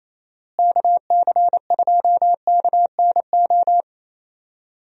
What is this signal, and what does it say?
Morse code, heard as a keyed tone, about 27 wpm, 700 Hz. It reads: KC2KNO
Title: USA Call Signs - 6 Characters - Repeated 2x - 28wpm